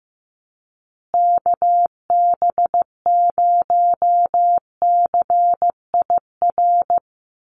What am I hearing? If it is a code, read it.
Morse code, heard as a keyed tone, about 15 wpm, 700 Hz. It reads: KB0CIR
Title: USA Call Signs Repeated 3x 15wpm